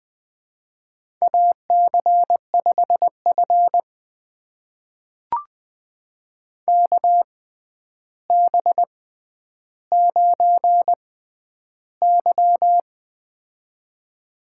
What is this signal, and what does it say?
Morse code, heard as a keyed tone, about 20 wpm, spaced as for 10 wpm, 700 Hz. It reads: AC5F KB9Y
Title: USA Call Signs - 4 Characters - 6x Character Spacing - 20wpm